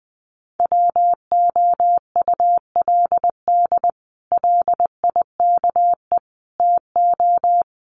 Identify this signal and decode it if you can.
Morse code, heard as a keyed tone, about 20 wpm, 700 Hz. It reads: WOULD LIKE TO
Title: Sets of 4 Words 20wpm